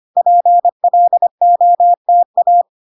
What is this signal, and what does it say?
Morse code, heard as a keyed tone, about 25 wpm, 700 Hz. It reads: PLOTA
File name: 25wpm